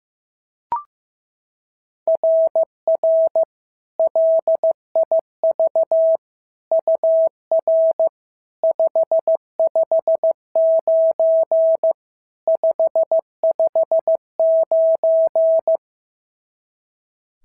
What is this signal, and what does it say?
Morse code, heard as a keyed tone, about 14 wpm, 650 Hz. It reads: RR LIV UR 559 559
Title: SOTA QSO Practice 15wpm